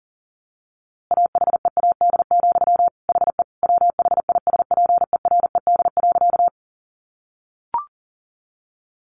Morse code, heard as a keyed tone, about 40 wpm, 700 Hz. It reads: AHEAD, HE WHISPERED.
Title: A Sound Of Thunder 40wpm